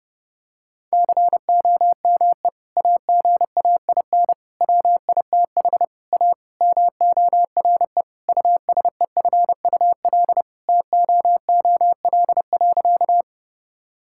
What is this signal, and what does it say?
Morse code, heard as a keyed tone, about 30 wpm, 700 Hz. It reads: COME AGAIN WITH A MORE USEFUL TOOL.
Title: Sentences from Top 1000 Words - Review Part 6 - 30wpm